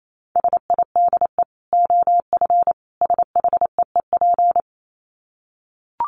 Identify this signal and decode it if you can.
Morse code, heard as a keyed tone, about 28 wpm, 700 Hz. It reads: SIDE OF SHEEP
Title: Dit Challenge 28wpm